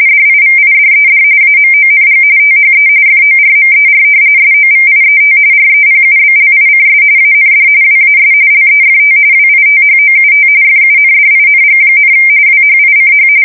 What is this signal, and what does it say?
RTTY, about 45.45 baud. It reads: DIVIDED AMONG THE FORMER SOVIET REPUBLICS AND WAS LEFT WITHOUT FUNDING, WHICH
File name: fsk170 / usnavy1